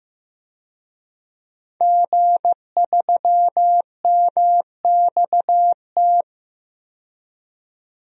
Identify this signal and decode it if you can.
Morse code, heard as a keyed tone, about 16 wpm, 700 Hz. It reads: G3MXT
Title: DXCC-100 Call Signs - 5 Characters - 15wpm